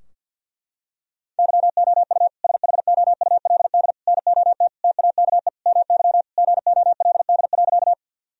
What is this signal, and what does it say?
Morse code, heard as a keyed tone, about 46 wpm, 700 Hz. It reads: YOU SHOULD NOT TAKE MY GOLD.
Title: Sentences from Top 1000 Words - Review Part 3 - 50wpm